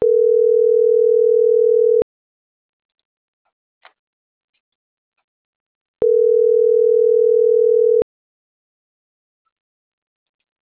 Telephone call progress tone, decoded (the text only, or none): ringback tone